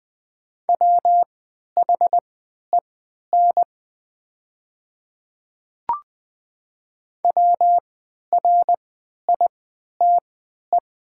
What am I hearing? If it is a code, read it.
Morse code, heard as a keyed tone, about 20 wpm, spaced as for 11 wpm, 700 Hz. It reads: WHEN WRITE
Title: 3x Character Spacing - Top 100 Words - 20wpm